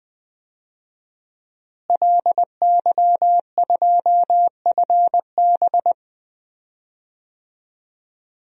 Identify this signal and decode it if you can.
Morse code, heard as a keyed tone, about 20 wpm, 700 Hz. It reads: LY2FB
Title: DXCC-100 Call Signs - 5 Characters - Repeated 2x - 20wpm